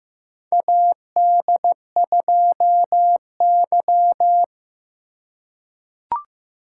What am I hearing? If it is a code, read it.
Morse code, heard as a keyed tone, about 15 wpm, 700 Hz. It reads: AD2Y
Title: USA Call Signs - 4 Characters - 4x Character Spacing - 15wpm